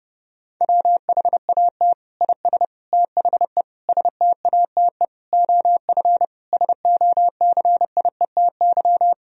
Morse code, heard as a keyed tone, about 32 wpm, 700 Hz. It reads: WHAT IS THE STATE OF SOCIETY
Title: Sentences from Top 1000 Words - Review Part 4 - 30wpm